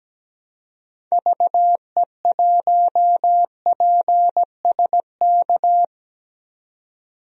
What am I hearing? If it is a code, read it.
Morse code, heard as a keyed tone, about 17 wpm, 700 Hz. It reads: VE1PSK
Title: DXCC-100 Call Signs - 6 Characters - 17wpm